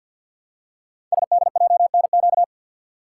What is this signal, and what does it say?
Morse code, heard as a keyed tone, about 52 wpm, 700 Hz. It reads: SDJNQ
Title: Code Groups - 5 Letter:Numbers - 50wpm